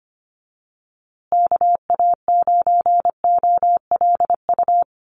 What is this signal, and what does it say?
Morse code, heard as a keyed tone, about 25 wpm, 700 Hz. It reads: KA9OLU
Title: USA Call Signs Repeated 2x 25wpm